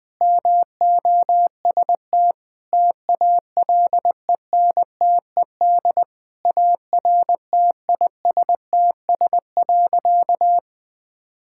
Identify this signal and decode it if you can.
Morse code, heard as a keyed tone, about 20 wpm, 700 Hz. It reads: MOST TALENTED ARTISTS.
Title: Sentences with 8-Word Phrases - 20wpm